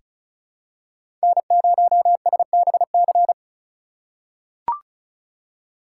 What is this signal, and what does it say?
Morse code, heard as a keyed tone, about 34 wpm, 700 Hz. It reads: N0SBC
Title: USA Call Signs - 5 Characters - 2x Character Spacing - 35wpm